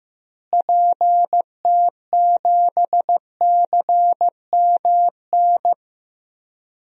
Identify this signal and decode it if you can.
Morse code, heard as a keyed tone, about 15 wpm, 700 Hz. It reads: PT7CMN
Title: DXCC-100 Call Signs - 6 Characters - Repeated 2x - 15wpm